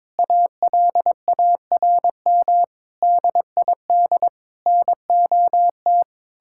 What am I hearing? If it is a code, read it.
Morse code, heard as a keyed tone, about 22 wpm, 700 Hz. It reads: ALARM DID NOT
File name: practice / 22wpm / Sentences with 4-Word Phrases - 22wpm